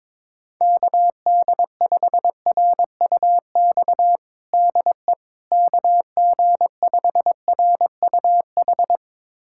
Morse code, heard as a keyed tone, about 22 wpm, 700 Hz. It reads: KD5RUX DE KG5RUH
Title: USA Callsign DE Callsign - 6 Characters - 22wpm